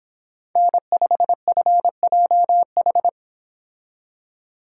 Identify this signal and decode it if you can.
Morse code, heard as a keyed tone, about 29 wpm, 700 Hz. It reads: N5FJH